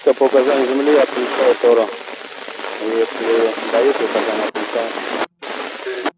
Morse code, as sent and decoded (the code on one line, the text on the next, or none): none